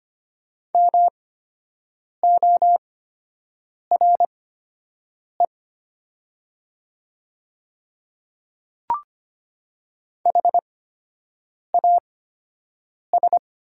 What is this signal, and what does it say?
Morse code, heard as a keyed tone, about 25 wpm, spaced as for 7 wpm, 700 Hz. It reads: MORE HAS
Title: 8x Character Spacing - Top 100 Words - 25wpm